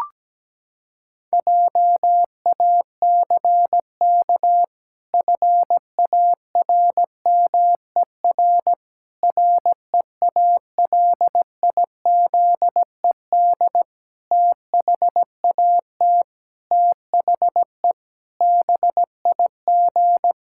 Morse code, heard as a keyed tone, about 17 wpm, 700 Hz. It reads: JACK FARMER REALIZED THAT THE BIG
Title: Pangram Challenge v2 - 17wpm